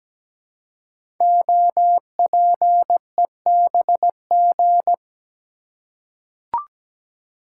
Morse code, heard as a keyed tone, about 17 wpm, 700 Hz. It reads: OPEBG
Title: Code Groups - 5 Letter:Numbers - 17wpm